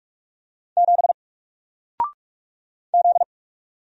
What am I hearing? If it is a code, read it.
Morse code, heard as a keyed tone, about 44 wpm, 700 Hz. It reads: 7 Z
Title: Single Character (Letter:Number:Punctuation:Prosign) 45wpm